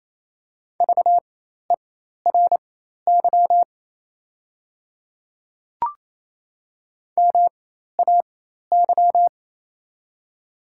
Morse code, heard as a keyed tone, about 27 wpm, spaced as for 13 wpm, 690 Hz. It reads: VERY MAY
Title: 4x Character Spacing - Top 200 Words - 28wpm